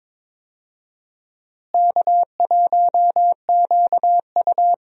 Morse code, heard as a keyed tone, about 21 wpm, 700 Hz. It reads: K1QU
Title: USA Call Signs - 4 Characters - Repeated 3x - 22wpm